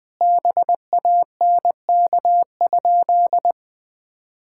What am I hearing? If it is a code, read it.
Morse code, heard as a keyed tone, about 20 wpm, 700 Hz. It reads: BANK?
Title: Sentences from Top 1000 Words - Review Part 6 - 20wpm